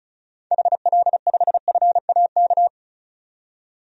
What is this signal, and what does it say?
Morse code, heard as a keyed tone, about 36 wpm, 700 Hz. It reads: HL5FAK
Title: DXCC-100 Call Signs - 6 Characters - 35wpm